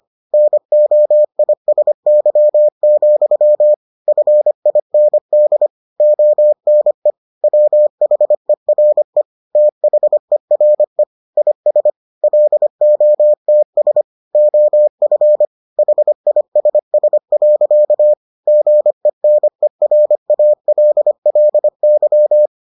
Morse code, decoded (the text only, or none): NOISY, FIND ONE WHERE THERE IS LOTS OF HISS. GENERALLY